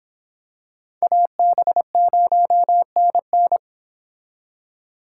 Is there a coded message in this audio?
Yes: Morse code from 1.0 to 3.6 s, heard as a keyed tone, about 26 wpm, 700 Hz. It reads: AB0NN